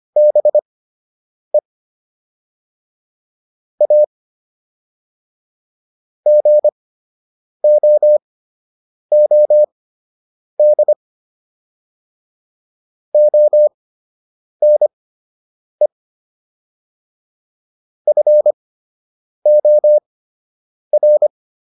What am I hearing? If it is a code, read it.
Morse code, heard as a keyed tone, about 25 wpm, spaced as for 8 wpm, 600 Hz. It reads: BE A GOOD ONE FOR